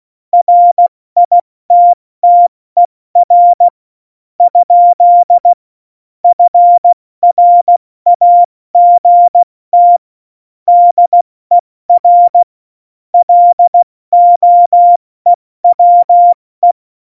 Morse code, heard as a keyed tone, about 16 wpm, spaced as for 13 wpm, 700 Hz. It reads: RITTER? FRAGT DER LOEWE
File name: Morsekurs / Witze / Witz 15. Mai 2018, 70 F80